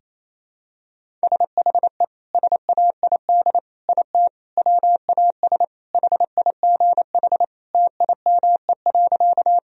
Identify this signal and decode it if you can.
Morse code, heard as a keyed tone, about 28 wpm, 700 Hz. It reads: SHE SAID IT WAS HIGH TIME.
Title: Sentences from Top 1000 Words - Review Part 1 - 28wpm